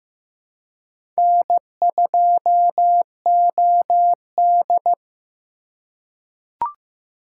Morse code, heard as a keyed tone, about 15 wpm, 700 Hz. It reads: N2OD
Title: USA Call Signs - 3x Character Spacing - 15wpm